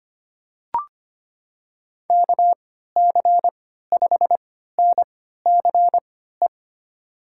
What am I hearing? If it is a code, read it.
Morse code, heard as a keyed tone, about 25 wpm, spaced as for 14 wpm, 700 Hz. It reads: KC5NCE
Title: USA Call Signs - 6 Characters - 3x Character Spacing - 25wpm